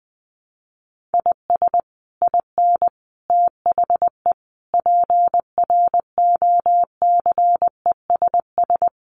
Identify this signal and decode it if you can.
Morse code, heard as a keyed tone, about 20 wpm, 700 Hz. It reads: IS IN THE PROCESS
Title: Sets of 5 Words Encore2 20wpm